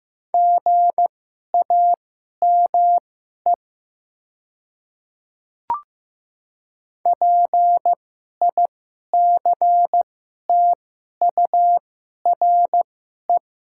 Morse code, heard as a keyed tone, about 15 wpm, spaced as for 11 wpm, 700 Hz. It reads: GAME PICTURE
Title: 2x Character Spacing - Top 500 Words - 15wpm